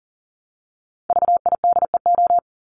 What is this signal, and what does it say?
Morse code, heard as a keyed tone, about 40 wpm, 700 Hz. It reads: VIDEO
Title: Top 501-750 Words 40wpm